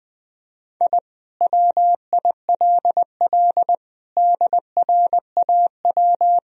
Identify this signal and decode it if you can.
Morse code, heard as a keyed tone, about 20 wpm, 700 Hz. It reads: I WILL DRAW
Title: Sentences from Top 1000 Words - Review Part 1 - 20wpm